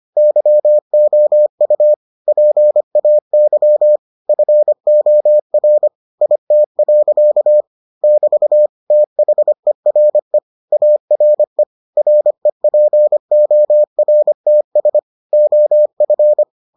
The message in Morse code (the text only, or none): YOU PAY FOR IT. <BT> THERE ARE REPORTS OF